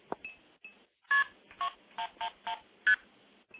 Multi-frequency trunk signalling, heard as fast pairs of tones, tones of about 90 ms. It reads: KP 111ST